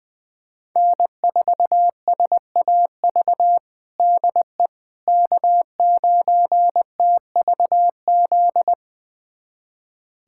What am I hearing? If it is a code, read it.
Morse code, heard as a keyed tone, about 20 wpm, 700 Hz. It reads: N4SAV DE K9TVZ
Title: USA Callsign DE Callsign - 5 Characters - 20wpm